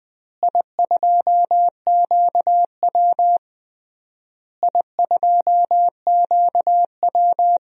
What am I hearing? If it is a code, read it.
Morse code, heard as a keyed tone, about 20 wpm, 700 Hz. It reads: I2QW I2QW
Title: DXCC-100 Call Signs - 4 Characters - Repeated 3x - 20wpm